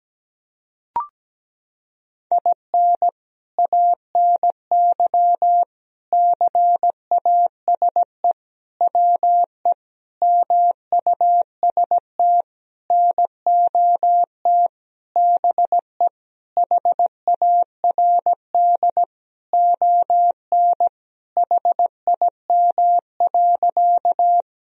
Morse code, heard as a keyed tone, about 17 wpm, 700 Hz. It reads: IN ANY CASE WE MUST NOT BE HARD ON HIM.
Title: Sentences from Top 1000 Words - Review Part 1 - 17wpm